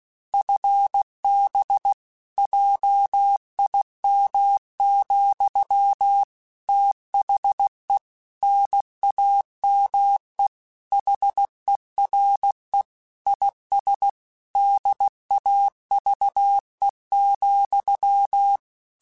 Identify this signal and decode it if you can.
Morse code, heard as a keyed tone, about 16 wpm, 800 Hz. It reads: FB JIM, THE NAME HERE IS DAVE,
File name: QSOs / QSOs-mp3s / WD0BGP 1@15